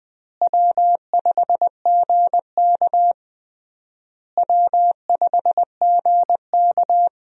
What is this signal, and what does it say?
Morse code, heard as a keyed tone, about 20 wpm, 700 Hz. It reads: W5GK W5GK
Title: USA Call Signs - 4 Characters - Repeated 3x - 20wpm